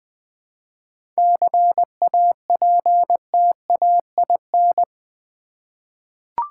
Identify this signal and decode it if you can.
Morse code, heard as a keyed tone, about 20 wpm, 690 Hz. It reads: CAPTAIN
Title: 7 Letter Words - First 3 Letters Given - 20wpm